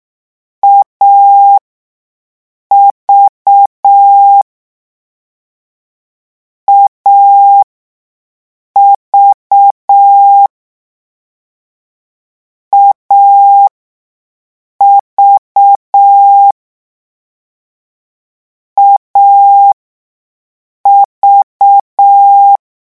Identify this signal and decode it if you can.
Morse code, heard as a keyed tone, about 6 wpm, spaced as for 5 wpm, 800 Hz. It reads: AV AV AV AV
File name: morse mp3 / lecon3-vitesse 6